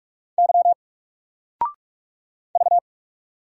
Morse code, heard as a keyed tone, about 44 wpm, 700 Hz. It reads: Y V
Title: Single Character (Letter:Number:Punctuation:Prosign) 45wpm